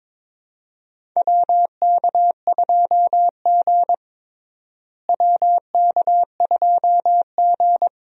Morse code, heard as a keyed tone, about 22 wpm, 700 Hz. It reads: WK2G WK2G
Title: USA Call Signs - 4 Characters - Repeated 3x - 22wpm